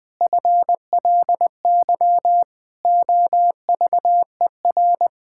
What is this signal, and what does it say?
Morse code, heard as a keyed tone, about 20 wpm, 700 Hz. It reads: FLY OVER
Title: Pangram Challenge v2 - 20wpm